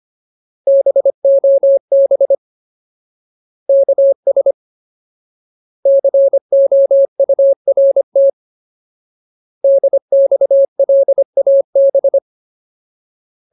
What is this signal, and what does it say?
Morse code, heard as a keyed tone, about 25 wpm, 550 Hz. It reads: BOB KS COURT DXLAB
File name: Ultimate Word List / 25WPM Ultimate